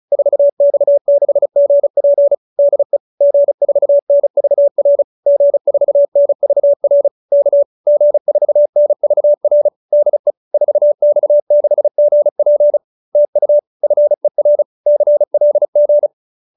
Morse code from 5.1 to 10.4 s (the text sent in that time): G4NVR K G4NVR DE